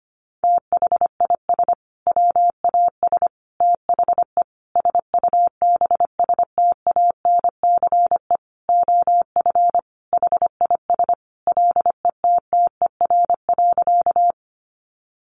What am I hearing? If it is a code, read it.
Morse code, heard as a keyed tone, about 25 wpm, 700 Hz. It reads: THIS WAS THE SUBSTANCE OF HIS LETTER.
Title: Sentences from Top 1500 Words 25wpm